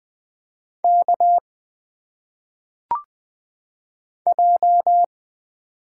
Morse code, heard as a keyed tone, about 20 wpm, 700 Hz. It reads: K J
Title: Single Letters - Phonetic Pronunciation 20wpm